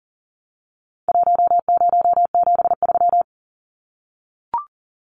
Morse code, heard as a keyed tone, about 40 wpm, 700 Hz. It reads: 1073
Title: Numbers - 4-Digit 40wpm